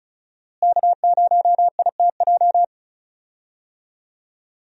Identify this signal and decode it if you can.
Morse code, heard as a keyed tone, about 36 wpm, 700 Hz. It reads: K0ITJ
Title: USA Call Signs - 5 Characters - Repeated 3x - 35wpm